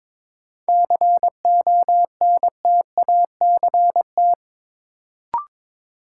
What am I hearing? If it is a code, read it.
Morse code, heard as a keyed tone, about 22 wpm, 700 Hz. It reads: CONTACT
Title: Sentences from Top 1000 Words - Review Part 7 - New Word Drill - 22wpm